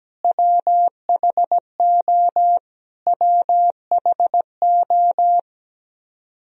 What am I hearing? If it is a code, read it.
Morse code, heard as a keyed tone, about 17 wpm, 700 Hz. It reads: WHO WHO
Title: Top 100 Words Repeated 17wpm